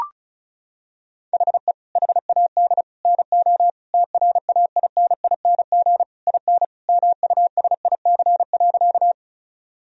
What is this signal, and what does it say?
Morse code, heard as a keyed tone, about 34 wpm, 700 Hz. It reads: HE HAD NO TRAINING IN MUSIC.
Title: Sentences from Top 1000 Words - Review Part 4 - 35wpm